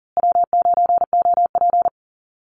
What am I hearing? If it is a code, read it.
Morse code, heard as a keyed tone, about 40 wpm, 700 Hz. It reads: W9OP
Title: Call Signs Repeated 2x - 40wpm